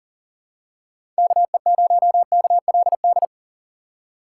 Morse code, heard as a keyed tone, about 40 wpm, 700 Hz. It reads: KE0KLD
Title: USA Call Signs - 6 Characters - 2x Character Spacing - 40wpm